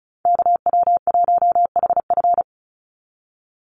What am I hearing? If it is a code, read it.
Morse code, heard as a keyed tone, about 34 wpm, 700 Hz. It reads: KW1HF
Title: USA Call Signs 35wpm